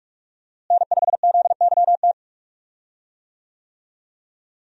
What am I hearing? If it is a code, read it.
Morse code, heard as a keyed tone, about 42 wpm, 700 Hz. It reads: N5ZYT
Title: USA Call Signs - 5 Characters - Repeated 2x - 45wpm